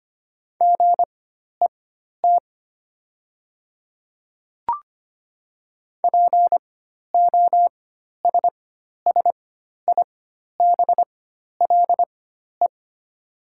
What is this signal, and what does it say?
Morse code, heard as a keyed tone, about 25 wpm, spaced as for 12 wpm, 700 Hz. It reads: GET POSSIBLE
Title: 4x Character Spacing - Top 500 Words - 25wpm